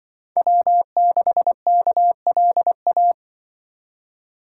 Morse code, heard as a keyed tone, about 25 wpm, 700 Hz. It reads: W6KLA
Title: Speed Racing - USA 1x3 Call Signs 20wpm